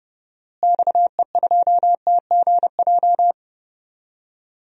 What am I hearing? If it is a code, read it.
Morse code, heard as a keyed tone, about 31 wpm, 700 Hz. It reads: XE2TGJ
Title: DXCC-100 Call Signs - 6 Characters - Repeated 2x - 30wpm